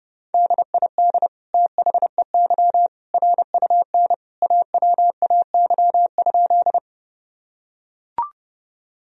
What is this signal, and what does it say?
Morse code, heard as a keyed tone, about 30 wpm, 700 Hz. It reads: DID THEY RUN AWAY?
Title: Sentences from Top 1000 Words - Review Part 1 - 30wpm